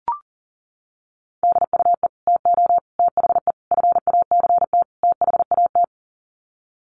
Morse code, heard as a keyed tone, about 40 wpm, 700 Hz. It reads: DUE TO THE FACT THAT
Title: Sets of 5 Words Encore 40wpm